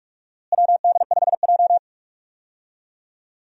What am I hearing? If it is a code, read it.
Morse code, heard as a keyed tone, about 46 wpm, 700 Hz. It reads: WD5J
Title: USA Call Signs - 4 Characters - Repeated 2x - 45wpm